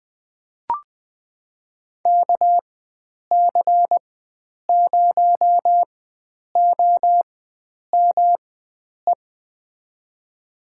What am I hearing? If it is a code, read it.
Morse code, heard as a keyed tone, about 20 wpm, spaced as for 9 wpm, 700 Hz. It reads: KC0OME
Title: USA Call Signs - 6 Characters - 4x Character Spacing - 20wpm